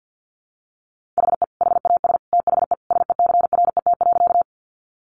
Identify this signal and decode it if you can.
Morse code, heard as a keyed tone, about 50 wpm, 700 Hz. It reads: HE HAS THE SECRET.
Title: Sentences from Top 1200 Words 50wpm